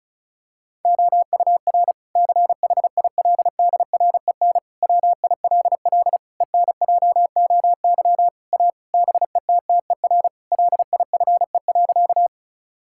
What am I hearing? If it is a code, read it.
Morse code, heard as a keyed tone, about 36 wpm, 700 Hz. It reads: OUR CHILDREN WILL ENJOY A BETTER LIFE.
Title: Sentences from Top 1000 Words - Review Part 5 - 35wpm